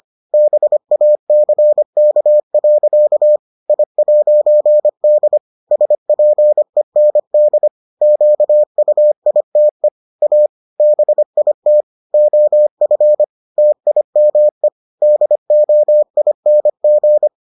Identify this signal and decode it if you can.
Morse code, heard as a keyed tone, about 25 wpm, 600 Hz. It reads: BACK. I'D SPEND QUITE A BIT OF TIME DOING